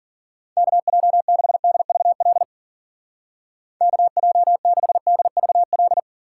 Morse code, heard as a keyed tone, about 50 wpm, 700 Hz. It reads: KJ6DVL KJ6DVL
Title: Speed Racing - USA Call Signs 40wpm